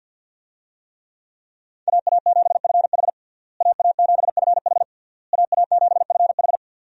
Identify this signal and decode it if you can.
Morse code, heard as a keyed tone, about 47 wpm, 700 Hz. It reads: AA7FH AA7FH AA7FH
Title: USA Call Signs - 5 Characters - Repeated 3x - 50wpm